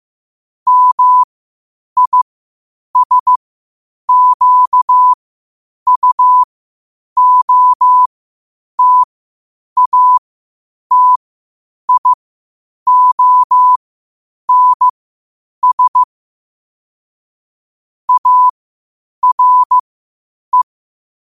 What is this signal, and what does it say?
Morse code, heard as a keyed tone, about 15 wpm, spaced as for 8 wpm, 1000 Hz. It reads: MISQUOTATIONS ARE
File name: qotdfiles / 8wpm